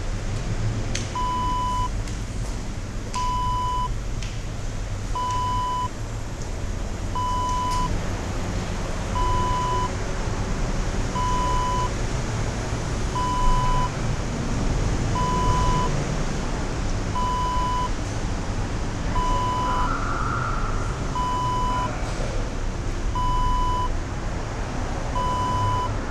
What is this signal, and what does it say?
There is no code.